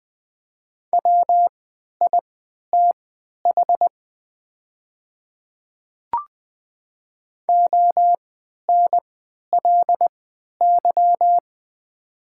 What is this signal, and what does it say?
Morse code, heard as a keyed tone, about 20 wpm, spaced as for 11 wpm, 700 Hz. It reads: WITH ONLY
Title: 3x Character Spacing - Top 200 Words - 20wpm